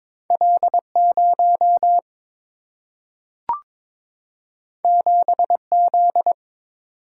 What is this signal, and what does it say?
Morse code, heard as a keyed tone, about 22 wpm, 700 Hz. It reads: L0 7Z
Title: Single-Letter Single-Number 22wpm